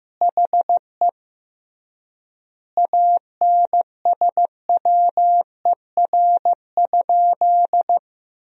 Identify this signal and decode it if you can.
Morse code, heard as a keyed tone, about 14 wpm, 700 Hz. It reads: HE ANSWER?
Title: 3x Word-Spacing - Sentences from Top 300 Words - 15wpm